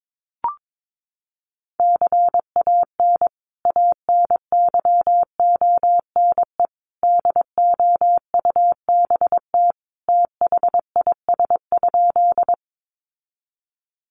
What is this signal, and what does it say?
Morse code, heard as a keyed tone, about 22 wpm, 700 Hz. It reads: CAN ANYONE DOUBT THIS?
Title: Sentences from Top 1500 Words 22wpm